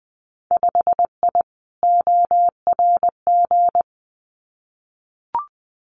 Morse code, heard as a keyed tone, about 20 wpm, 700 Hz. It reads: 5I ORG
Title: Winter Field Day Exchanges Encore 20wpm